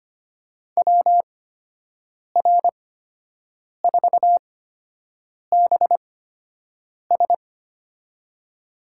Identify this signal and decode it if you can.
Morse code, heard as a keyed tone, about 25 wpm, spaced as for 7 wpm, 700 Hz. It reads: WR4BS
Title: USA Call Signs - 5 Characters - 8x Character Spacing - 25wpm